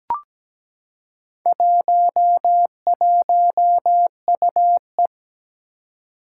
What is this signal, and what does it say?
Morse code, heard as a keyed tone, about 17 wpm, 700 Hz. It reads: 11UE
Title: Four Challenging Letter-Number Combinations 17wpm